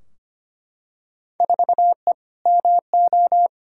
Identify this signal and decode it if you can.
Morse code, heard as a keyed tone, about 25 wpm, 700 Hz. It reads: 4E MO
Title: Field Day Exchanges Encore - 25wpm